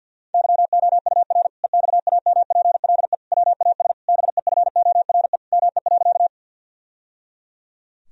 Morse code, heard as a keyed tone, about 52 wpm, 700 Hz. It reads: YOUR EXAMPLE WAS BEFORE ME.
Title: Sentences from Top 1000 Words - Review Part 1 - 50wpm